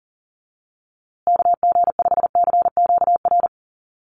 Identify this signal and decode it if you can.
Morse code, heard as a keyed tone, about 40 wpm, 700 Hz. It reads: KG5CQR
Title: USA Call Signs Repeated 3x 40wpm